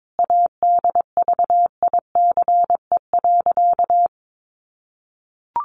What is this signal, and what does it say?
Morse code, heard as a keyed tone, about 22 wpm, 700 Hz. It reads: ADVICE.
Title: Sentences from Top 1100 Words 22wpm